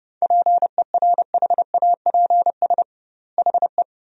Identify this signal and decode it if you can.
Morse code, heard as a keyed tone, about 31 wpm, 700 Hz. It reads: PERHAPS HE
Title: 2x Word-Spacing - Sentences from Top 500 Words - 30wpm